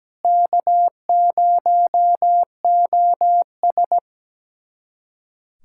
Morse code, heard as a keyed tone, about 17 wpm, 700 Hz. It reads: K0OS
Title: USA Callsign DE Callsign - 4 Characters - 17wpm